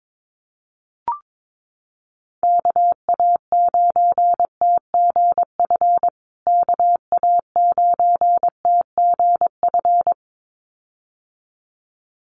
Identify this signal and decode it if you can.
Morse code, heard as a keyed tone, about 22 wpm, 700 Hz. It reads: KA9TGF KA9TGF
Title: USA Call Signs Repeated 2x 22wpm